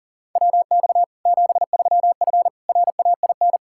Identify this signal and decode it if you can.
Morse code, heard as a keyed tone, about 40 wpm, 700 Hz. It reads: WX 73F RAIN
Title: Practice QSO - Exchange 2 - Info - 40wpm